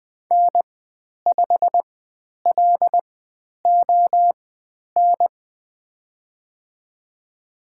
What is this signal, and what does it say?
Morse code, heard as a keyed tone, about 20 wpm, spaced as for 10 wpm, 700 Hz. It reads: N5LON
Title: 102 Lesson 17g - Letter u - Call Signs 20wpm